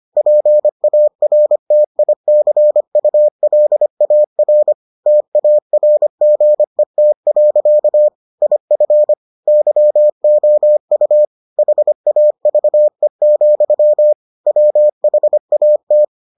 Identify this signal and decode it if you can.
Morse code, heard as a keyed tone, about 25 wpm, 600 Hz. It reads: PARTICULAR TARGET. IF YOU HAVE, WHAT